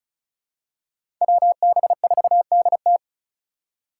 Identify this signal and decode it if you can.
Morse code, heard as a keyed tone, about 34 wpm, 700 Hz. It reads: WB4DT
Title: USA Call Signs - 5 Characters - 4x Character Spacing - 35wpm